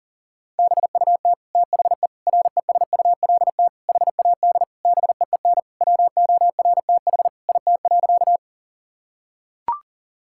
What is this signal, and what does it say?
Morse code, heard as a keyed tone, about 37 wpm, 700 Hz. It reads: BUT THE RESULT HAD BEEN WORTH IT.
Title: Sentences from Top 1000 Words - Review Part 5 - 40wpm